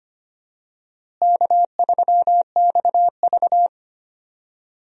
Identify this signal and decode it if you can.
Morse code, heard as a keyed tone, about 25 wpm, 700 Hz. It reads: K3XV
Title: USA Call Signs - 4 Characters - Repeated 2x - 25wpm